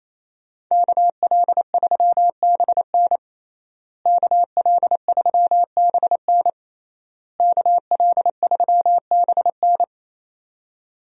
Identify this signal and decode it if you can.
Morse code, heard as a keyed tone, about 28 wpm, 700 Hz. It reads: KL3BN KL3BN KL3BN
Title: USA Call Signs - 5 Characters - Repeated 3x - 28wpm